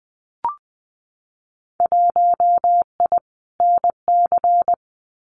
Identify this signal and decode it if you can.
Morse code, heard as a keyed tone, about 20 wpm, 700 Hz. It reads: 1I NC
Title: Winter Field Day Exchanges 20wpm